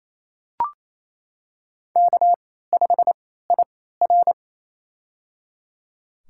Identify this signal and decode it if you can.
Morse code, heard as a keyed tone, about 28 wpm, spaced as for 16 wpm, 700 Hz. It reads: K5IR
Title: USA Call Signs - 3x Character Spacing - 28wpm